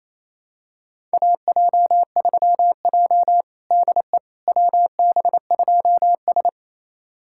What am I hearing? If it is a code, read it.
Morse code, heard as a keyed tone, about 28 wpm, 700 Hz. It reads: AJ3J DE WB2S
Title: USA Callsign DE Callsign - 4 Characters - 28wpm